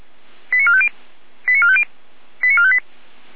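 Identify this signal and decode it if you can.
Five-tone selective call (ZVEI), heard as quick runs of five tones, tones of about 70 ms. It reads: 89450 89450 89458
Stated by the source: ZVEI five-tone selective call 89457 received by SDR.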